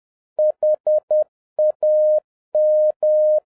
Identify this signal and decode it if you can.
Morse code, heard as a keyed tone, about 10 wpm, 610 Hz. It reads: HAM